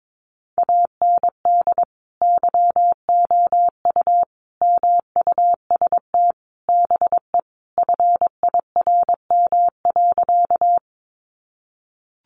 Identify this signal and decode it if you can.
Morse code, heard as a keyed tone, about 22 wpm, 700 Hz. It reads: AND YOU MUST BE FIRM.
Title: Sentences from Top 1400 Words 22wpm